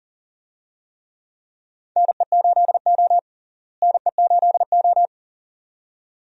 Morse code, heard as a keyed tone, about 40 wpm, 700 Hz. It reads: NE8O NE8O
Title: USA Call Signs - 4 Characters - Repeated 2x - 40wpm